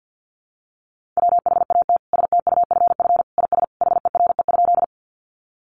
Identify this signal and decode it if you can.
Morse code, heard as a keyed tone, about 50 wpm, 700 Hz. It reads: WHAT STUFF IS HERE?
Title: Sentences from Top 1400 Words 50wpm